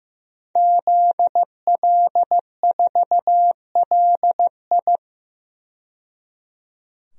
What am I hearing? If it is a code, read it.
Morse code, heard as a keyed tone, about 15 wpm, 700 Hz. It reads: ZL4LI
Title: DXCC-100 Call Signs - 5 Characters - 15wpm